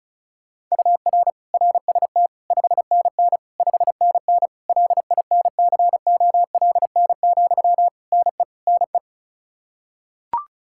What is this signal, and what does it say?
Morse code, heard as a keyed tone, about 36 wpm, 700 Hz. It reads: UR RST 5NN 5NN LINCOLN, NE NE
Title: Practice QSO - Exchange 1 - Beginning - 35wpm